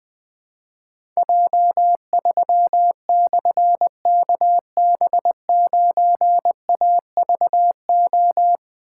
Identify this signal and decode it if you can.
Morse code, heard as a keyed tone, about 20 wpm, 700 Hz. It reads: J3/KB9AVO
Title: Difficult Call Signs 20wpm